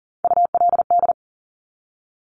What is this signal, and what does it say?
Morse code, heard as a keyed tone, about 40 wpm, 700 Hz. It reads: ULD
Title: Top Three Letter Combinations in Words 40wpm